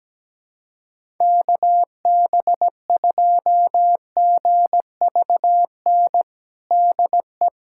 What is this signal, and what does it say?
Morse code, heard as a keyed tone, about 17 wpm, 700 Hz. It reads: KB2GVN DE
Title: USA Callsign DE Callsign - 6 Characters - 17wpm